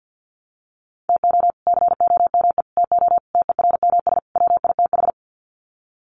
Morse code, heard as a keyed tone, about 50 wpm, 700 Hz. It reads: TO COME TO TERMS WITH
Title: Sets of 5 Words Encore 50wpm